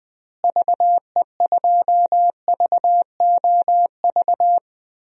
Morse code, heard as a keyed tone, about 20 wpm, 700 Hz. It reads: VE2VOV
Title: DXCC-100 Call Signs - 6 Characters - Repeated 3x - 20wpm